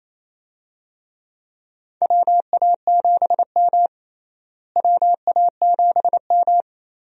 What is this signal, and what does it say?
Morse code, heard as a keyed tone, about 28 wpm, 700 Hz. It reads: WA7M WA7M
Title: USA Call Signs - 4 Characters - Repeated 2x - 28wpm